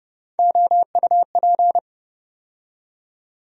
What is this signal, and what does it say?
Morse code, heard as a keyed tone, about 30 wpm, 700 Hz. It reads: OUP
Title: USA Call Sign Postfixes - 3 Letters 30wpm